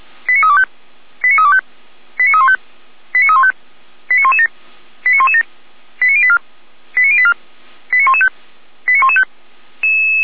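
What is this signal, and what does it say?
Five-tone selective call (ZVEI), heard as quick runs of five tones, tones of about 70 ms. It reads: 89325 89325 89315 89315 89117 89117 89084 89084 89115 89115
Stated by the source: ZVEI five-tone call received by SDR, from the audio server of the Orlamünde fire brigade (feuerwehr-orlamuende).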